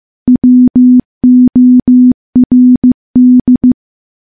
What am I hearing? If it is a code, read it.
Morse code, heard as a keyed tone, about 15 wpm, 250 Hz. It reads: WORD
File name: morse / morse-training-15WPM